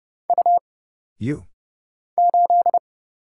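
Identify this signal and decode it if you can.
Morse code, heard as a keyed tone, about 30 wpm, 700 Hz. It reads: U 8
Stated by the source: Morse Code Ninja single-character practice (letters, numbers, punctuation, prosigns) at 30 wpm, by Kurt Zoglmann (AD0WE).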